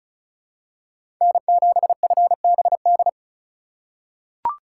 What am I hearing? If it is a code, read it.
Morse code, heard as a keyed tone, about 34 wpm, 700 Hz. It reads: N7FBD
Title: USA Call Signs - 5 Characters - 2x Character Spacing - 35wpm